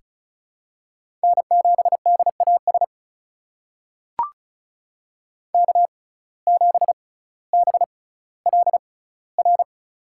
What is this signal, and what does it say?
Morse code, heard as a keyed tone, about 36 wpm, spaced as for 18 wpm, 700 Hz. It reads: N7DAS K7BLR
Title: USA Call Signs - 5 Characters - 6x Character Spacing - 35wpm